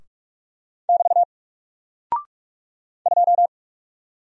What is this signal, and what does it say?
Morse code, heard as a keyed tone, about 44 wpm, 700 Hz. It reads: <BT> 2
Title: Single Character (Letter:Number:Punctuation:Prosign) 45wpm